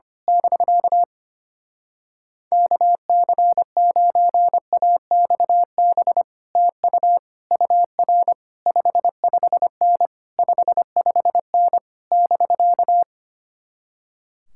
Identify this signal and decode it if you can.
Morse code, heard as a keyed tone, about 24 wpm, 700 Hz. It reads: <BK> KC9AXB TU UR 55N 55N <BK>
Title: Speed-Racing Common QSO Phrases Encore - 25wpm